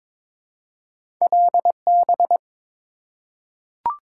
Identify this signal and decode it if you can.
Morse code, heard as a keyed tone, about 22 wpm, 700 Hz. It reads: LB
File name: practice / 22wpm / Two Letter-Number Combinations - 22wpm